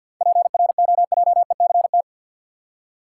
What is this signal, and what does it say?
Morse code, heard as a keyed tone, about 50 wpm, 700 Hz. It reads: PROJECT
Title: Seven Letter Words - 50wpm